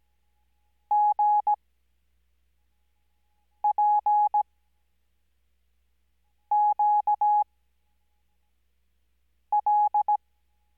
Morse code, heard as a keyed tone, about 17 wpm, spaced as for 4 wpm, 840 Hz. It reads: GPQL
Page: Farnsworth Spacing: 15WPM Character speed Spaced at 5WPM.